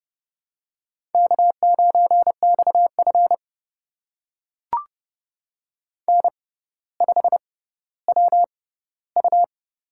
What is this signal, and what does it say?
Morse code, heard as a keyed tone, about 29 wpm, spaced as for 15 wpm, 700 Hz. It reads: K9XF N5WU
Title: USA Call Signs - 4 Characters - 6x Character Spacing - 30wpm